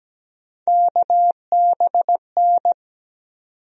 Morse code, heard as a keyed tone, about 17 wpm, 700 Hz. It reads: KBN
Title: USA Call Sign Postfixes - 3 Letters 17wpm